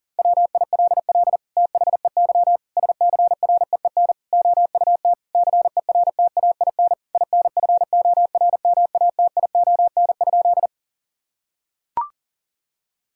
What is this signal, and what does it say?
Morse code, heard as a keyed tone, about 37 wpm, 700 Hz. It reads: WILL THEY SCREEN OUT CERTAIN INFORMATION?
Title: Sentences from Top 1000 Words - Review Part 6 - 40wpm